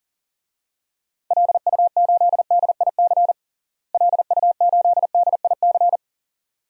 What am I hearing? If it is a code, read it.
Morse code, heard as a keyed tone, about 37 wpm, 700 Hz. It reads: LU8DIC LU8DIC
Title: DXCC-100 Call Signs - 6 Characters - Repeated 2x - 40wpm